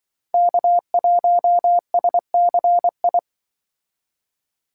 Morse code, heard as a keyed tone, about 25 wpm, 700 Hz. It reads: K1SCI
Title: Speed Racing - USA 1x3 Call Signs 20wpm